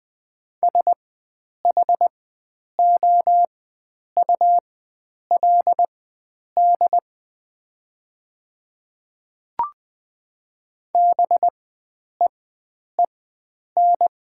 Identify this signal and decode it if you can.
Morse code, heard as a keyed tone, about 20 wpm, spaced as for 9 wpm, 700 Hz. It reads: SHOULD BEEN
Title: 4x Character Spacing - Top 300 Words - 20wpm